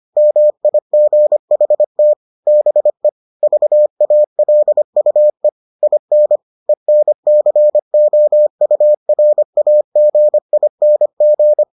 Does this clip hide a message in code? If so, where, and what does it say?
Yes: Morse code from 0.2 to 11.6 s, heard as a keyed tone, about 25 wpm, 600 Hz. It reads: MIGHT BE VALUE IN ENCOURAGING